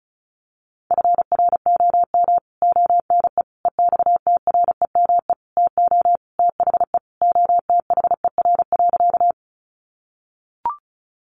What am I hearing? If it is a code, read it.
Morse code, heard as a keyed tone, about 34 wpm, 700 Hz. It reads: FROM ONE EXTREME TO THE OTHER.
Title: Sentences from Top 1300 Words 35wpm